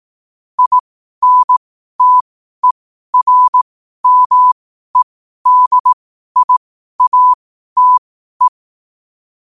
Morse code, heard as a keyed tone, about 18 wpm, spaced as for 12 wpm, 1000 Hz. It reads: INTERMEDIATE
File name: qotdfiles / 12wpm